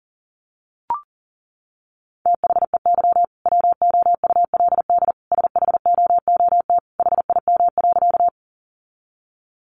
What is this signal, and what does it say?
Morse code, heard as a keyed tone, about 40 wpm, 700 Hz. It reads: THEY WOULD SHOOT HIM.
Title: Sentences from Top 1200 Words 40wpm